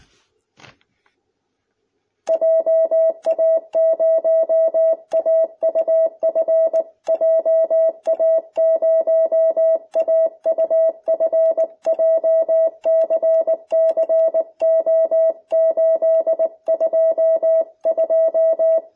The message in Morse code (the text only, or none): JA0AUF JA0AUF JCCO822